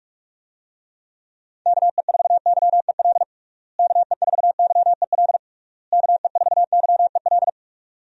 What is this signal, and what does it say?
Morse code, heard as a keyed tone, about 46 wpm, 700 Hz. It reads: KE4YEL KE4YEL KE4YEL
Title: USA Call Signs - 6 Characters - Repeated 3x - 45wpm